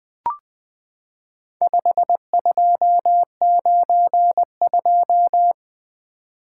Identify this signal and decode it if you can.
Morse code, heard as a keyed tone, about 20 wpm, 700 Hz. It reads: 5292
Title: Four Letter-Number Combinations - 20wpm